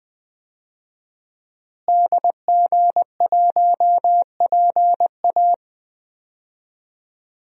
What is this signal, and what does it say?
Morse code, heard as a keyed tone, about 21 wpm, 700 Hz. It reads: DG1PA
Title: DXCC-100 Call Signs - 5 Characters - Repeated 2x - 20wpm